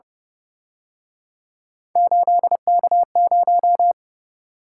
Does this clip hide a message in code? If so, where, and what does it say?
Yes: Morse code from 2.0 to 3.9 s, heard as a keyed tone, about 30 wpm, 700 Hz. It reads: 8K0